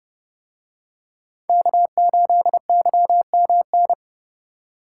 Morse code, heard as a keyed tone, about 30 wpm, 700 Hz. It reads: K8YMN